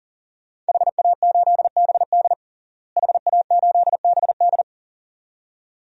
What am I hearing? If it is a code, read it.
Morse code, heard as a keyed tone, about 40 wpm, 700 Hz. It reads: HA8BD HA8BD
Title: DXCC-100 Call Signs - 5 Characters - Repeated 2x - 40wpm